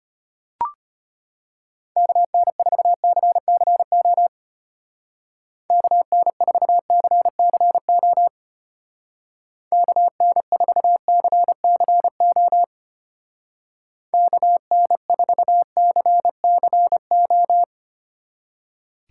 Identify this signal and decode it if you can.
Morse code, heard as a keyed tone, about 32 wpm, 700 Hz. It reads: KN4CCO KN4CCO KN4CCO KN4CCO